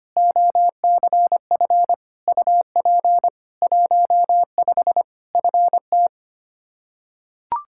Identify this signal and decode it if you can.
Morse code, heard as a keyed tone, about 25 wpm, 700 Hz. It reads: OCF UP 15 FT
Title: Practice QSO - Exchange 2 - Info - 25wpm